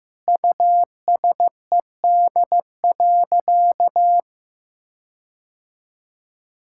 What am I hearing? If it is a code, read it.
Morse code, heard as a keyed tone, about 16 wpm, 700 Hz. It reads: USED.
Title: Sentences from Top 1600 Words 15wpm